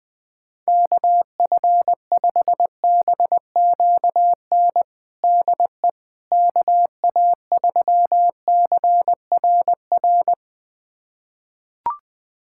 Morse code, heard as a keyed tone, about 20 wpm, 700 Hz. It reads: KF5BQN DE KA3CRR
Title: USA Callsign DE Callsign - 6 Characters - 20wpm